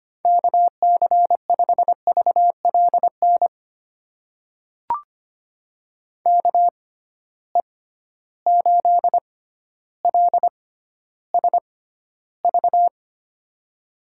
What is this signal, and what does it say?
Morse code, heard as a keyed tone, about 25 wpm, spaced as for 13 wpm, 700 Hz. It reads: KC5VLN KE8LSV
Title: USA Call Signs - 6 Characters - 6x Character Spacing - 25wpm